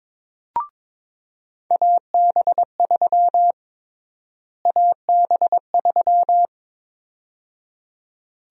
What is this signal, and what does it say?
Morse code, heard as a keyed tone, about 22 wpm, 700 Hz. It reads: AB3 AB3
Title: USA Call Sign Prefixes - Repeated 2x - 22wpm